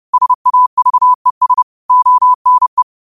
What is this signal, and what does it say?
Morse code, heard as a keyed tone, about 30 wpm, 1000 Hz. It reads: SAVES ONE